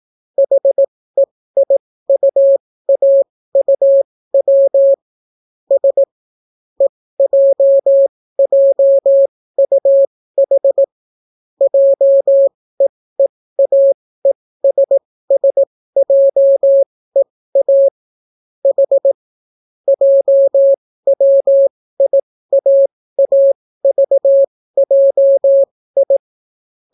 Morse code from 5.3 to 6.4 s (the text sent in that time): S